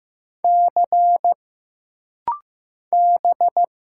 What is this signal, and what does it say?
Morse code, heard as a keyed tone, about 15 wpm, 700 Hz. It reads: C B
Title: Single Letters - Rapid-Fire 15wpm